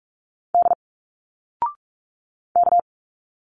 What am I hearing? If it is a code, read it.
Morse code, heard as a keyed tone, about 46 wpm, 700 Hz. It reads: D K
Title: Single Letters 45wpm